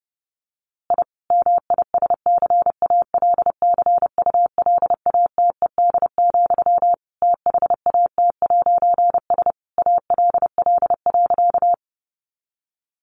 Morse code, heard as a keyed tone, about 30 wpm, 700 Hz. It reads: I MISCALCULATED, THAT'S ALL.
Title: A Sound Of Thunder 30wpm